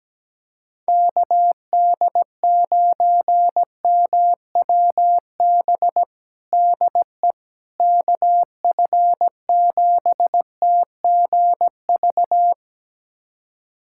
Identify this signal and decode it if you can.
Morse code, heard as a keyed tone, about 17 wpm, 700 Hz. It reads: KD9MWB DE KF7TGV
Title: USA Callsign DE Callsign - 6 Characters - 17wpm